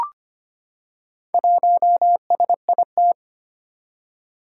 Morse code, heard as a keyed tone, about 25 wpm, 700 Hz. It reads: 1SIT
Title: Four Challenging Letter-Number Combinations 25wpm